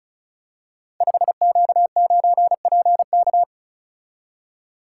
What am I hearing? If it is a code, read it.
Morse code, heard as a keyed tone, about 36 wpm, 690 Hz. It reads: 5Q9PK
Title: Code Groups - 5 Letter:Numbers - 35wpm